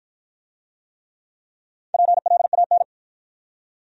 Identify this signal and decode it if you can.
Morse code, heard as a keyed tone, about 50 wpm, 700 Hz. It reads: PLAN